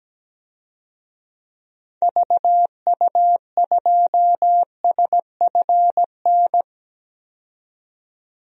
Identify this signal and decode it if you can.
Morse code, heard as a keyed tone, about 17 wpm, 700 Hz. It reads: VU2SFN
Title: DXCC-100 Call Signs - 6 Characters - 17wpm